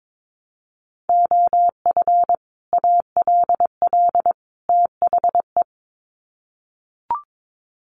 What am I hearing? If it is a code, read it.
Morse code, heard as a keyed tone, about 22 wpm, 700 Hz. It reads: OF ALL THE
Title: Sets of 3 Words 22wpm